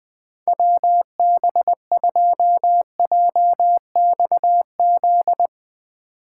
Morse code, heard as a keyed tone, about 20 wpm, 700 Hz. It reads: WB2JXZ
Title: USA Call Signs - 6 Characters - 6x Character Spacing - 20wpm